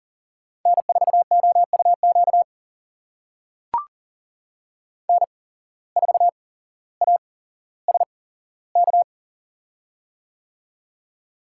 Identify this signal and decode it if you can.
Morse code, heard as a keyed tone, about 40 wpm, spaced as for 17 wpm, 700 Hz. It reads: N4OUQ N4ASK
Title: USA Call Signs - 5 Characters - 8x Character Spacing - 40wpm